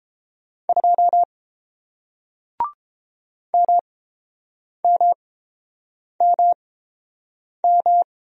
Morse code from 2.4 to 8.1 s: MMMM

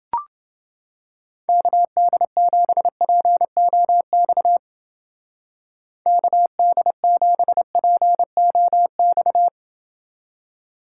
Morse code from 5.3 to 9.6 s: KD7POX

